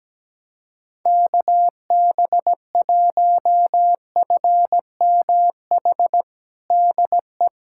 Morse code, heard as a keyed tone, about 17 wpm, 700 Hz. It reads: KB1FMH DE
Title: USA Callsign DE Callsign - 6 Characters - 17wpm